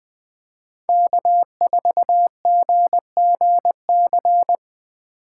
Morse code, heard as a keyed tone, about 20 wpm, 700 Hz. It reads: K4GGC